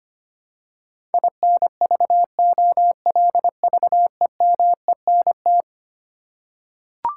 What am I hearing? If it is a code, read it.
Morse code, heard as a keyed tone, about 25 wpm, 700 Hz. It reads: INVOLVEMENT
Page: morse-code-ninja-practice-25wpm